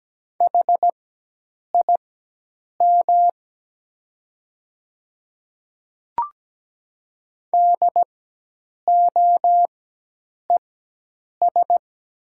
Morse code, heard as a keyed tone, about 17 wpm, spaced as for 8 wpm, 700 Hz. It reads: HIM DOES
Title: 4x Character Spacing - Top 200 Words - 17wpm